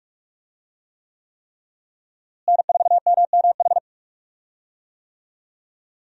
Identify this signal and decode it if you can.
Morse code, heard as a keyed tone, about 46 wpm, 700 Hz. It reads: N4MMH
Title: USA Call Signs - 5 Characters - Repeated 3x - 45wpm